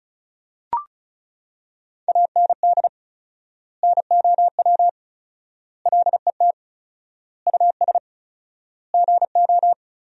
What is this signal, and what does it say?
Morse code, heard as a keyed tone, about 36 wpm, 700 Hz. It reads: AND NOW LET US GO
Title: 4x Word-Spacing - Sentences from Top 300 Words - 35wpm